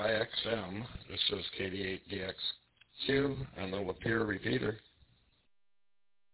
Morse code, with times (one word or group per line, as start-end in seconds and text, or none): none